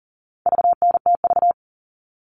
Morse code, heard as a keyed tone, about 40 wpm, 700 Hz. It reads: VNTV
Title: All Cut Numbers - 4-Digit 40wpm